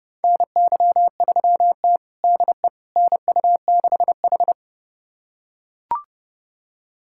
Morse code, heard as a keyed tone, about 30 wpm, 700 Hz. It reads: NY3T DE NU6H
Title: USA Callsign DE Callsign - 4 Characters - 30wpm